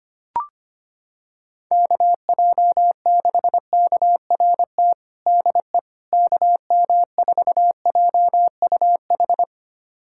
Morse code, heard as a keyed tone, about 25 wpm, 700 Hz. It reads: KJ6KRT DE KM4JUH
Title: USA Callsign DE Callsign 25wpm